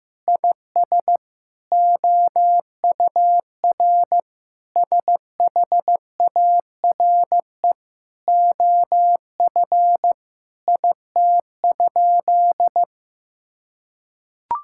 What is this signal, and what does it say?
Morse code, heard as a keyed tone, about 15 wpm, 700 Hz. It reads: IS OUR SHARE OF IT?
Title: Sentences from Top 1000 Words - Review Part 6 - 15wpm